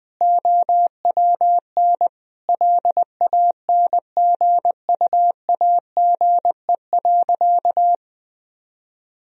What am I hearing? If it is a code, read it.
Morse code, heard as a keyed tone, about 20 wpm, 700 Hz. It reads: OWN LANGUAGE.
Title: Sentences from Top 1000 Words - Review Part 3 - 20wpm